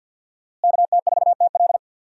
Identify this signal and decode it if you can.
Morse code, heard as a keyed tone, about 45 wpm, 700 Hz. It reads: KT4TL
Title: USA Call Signs - 5 Characters - Repeated 2x - 50wpm